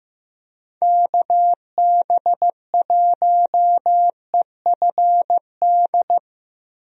Morse code, heard as a keyed tone, about 15 wpm, 700 Hz. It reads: KB1EFD
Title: USA Call Signs - 6 Characters - Repeated 2x - 15wpm